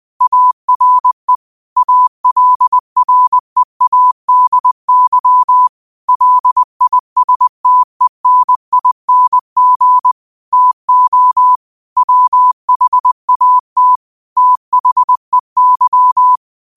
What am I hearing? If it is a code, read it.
Morse code, heard as a keyed tone, about 20 wpm, 1000 Hz. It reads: ARE ALREADY LISTENING TO WHAT THEY